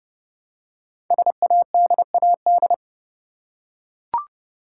Morse code, heard as a keyed tone, about 30 wpm, 700 Hz. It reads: SADAD
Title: Code Groups - 5 Letters - 30wpm